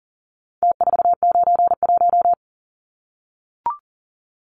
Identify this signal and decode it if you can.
Morse code, heard as a keyed tone, about 40 wpm, 700 Hz. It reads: T491
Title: Numbers - Contest Style - 4-Digit 40wpm